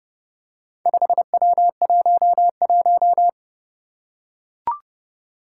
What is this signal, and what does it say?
Morse code, heard as a keyed tone, about 30 wpm, 700 Hz. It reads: 5W11
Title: Four Challenging Letter-Number Combinations 30wpm